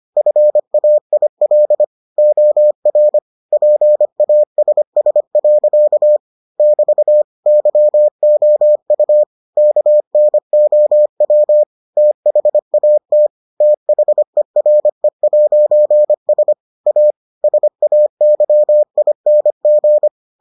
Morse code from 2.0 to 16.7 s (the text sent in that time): OR PASS. <BT> YOU KNOW THAT THERE'S